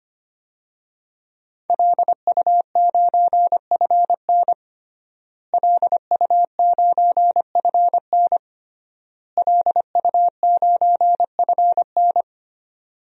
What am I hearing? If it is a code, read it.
Morse code, heard as a keyed tone, about 25 wpm, 700 Hz. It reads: LU9FN LU9FN LU9FN
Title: DXCC-100 Call Signs - 5 Characters - Repeated 3x - 25wpm